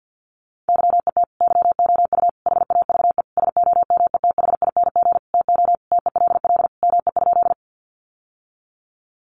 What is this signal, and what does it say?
Morse code, heard as a keyed tone, about 50 wpm, 700 Hz. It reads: YET YOU HAVE SOMETHING TO TELL ME?
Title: Sentences from Top 600 Words 50wpm